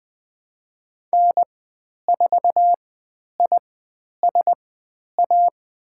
Morse code, heard as a keyed tone, about 20 wpm, spaced as for 10 wpm, 700 Hz. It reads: N4ISA
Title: 102 Lesson 17g - Letter u - Call Signs 20wpm